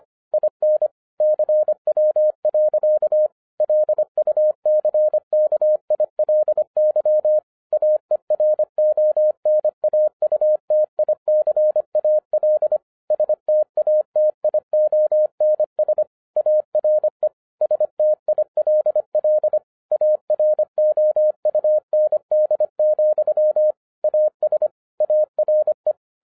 Morse code, heard as a keyed tone, about 25 wpm, 610 Hz. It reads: IN CW. LUCKILY AERONAUTICAL STATIONS ARE STILL AROUND, AS ARE